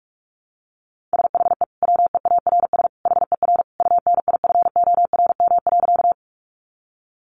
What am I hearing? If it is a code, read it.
Morse code, heard as a keyed tone, about 44 wpm, 700 Hz. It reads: SHE WEARS HER UNIFORM.
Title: Easy Phrases 45wpm